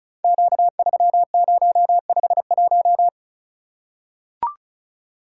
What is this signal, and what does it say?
Morse code, heard as a keyed tone, about 36 wpm, 700 Hz. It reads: Q3051
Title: Code Groups - 5 Letter:Numbers - 35wpm